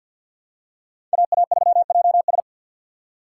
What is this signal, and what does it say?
Morse code, heard as a keyed tone, about 51 wpm, 700 Hz. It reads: AA3JS
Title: USA Call Signs - 5 Characters - 2x Character Spacing - 50wpm